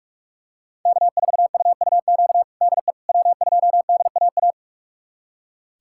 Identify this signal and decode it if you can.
Morse code, heard as a keyed tone, about 43 wpm, 700 Hz. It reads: K4UUQ DE W2DAA
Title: USA Callsign DE Callsign - 5 Characters - 45wpm